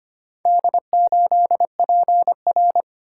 Morse code, heard as a keyed tone, about 25 wpm, 700 Hz. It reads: D8PR
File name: practice / 25wpm / Four Letter-Number Combinations - 25wpm